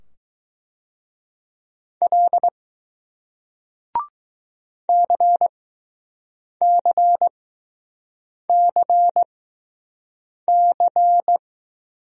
Morse code, heard as a keyed tone, about 20 wpm, spaced as for 6 wpm, 700 Hz. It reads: L CCCC